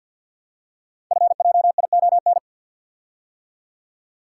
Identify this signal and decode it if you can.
Morse code, heard as a keyed tone, about 47 wpm, 700 Hz. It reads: FJION
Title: Code Groups - 5 Letters - 50wpm